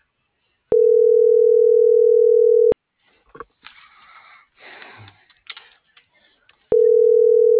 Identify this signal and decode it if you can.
Ringback tone (North America), 440 + 480 Hz.